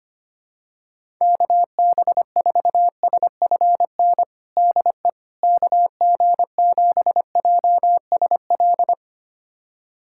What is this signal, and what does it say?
Morse code, heard as a keyed tone, about 25 wpm, 700 Hz. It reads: KB4SFN DE KG7JSL
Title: USA Callsign DE Callsign - 6 Characters - 25wpm